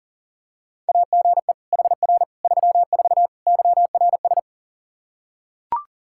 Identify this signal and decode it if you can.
Morse code, heard as a keyed tone, about 40 wpm, 700 Hz. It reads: AGE HR 34 YRS
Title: Practice QSO - Exchange 3 - More Info - 40wpm